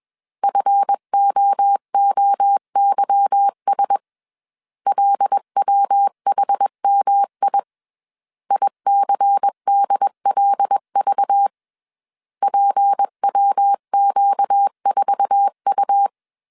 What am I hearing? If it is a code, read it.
Morse code, heard as a keyed tone, about 21 wpm, 800 Hz. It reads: FOOYS LWHMI ICDLV PWQ4U